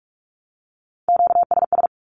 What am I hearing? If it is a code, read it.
Morse code, heard as a keyed tone, about 44 wpm, 700 Hz. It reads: QSS
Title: Q-Codes Repeated 3x 45wpm